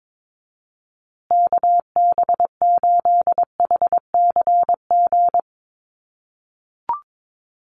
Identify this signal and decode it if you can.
Morse code, heard as a keyed tone, about 22 wpm, 700 Hz. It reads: KB8HCG
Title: USA Call Signs Repeated 2x 22wpm